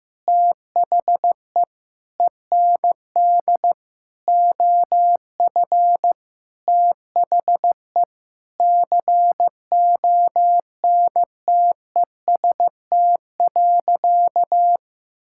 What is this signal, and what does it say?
Morse code, heard as a keyed tone, about 15 wpm, 700 Hz. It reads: THE END OF THE CONTEST.
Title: Sentences from Top 1600 Words 15wpm